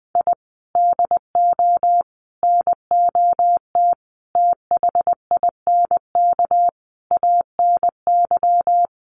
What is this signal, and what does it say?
Morse code, heard as a keyed tone, about 19 wpm, 700 Hz. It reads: I DO NOT THINK ANY
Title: Sets of 5 Words Encore 20wpm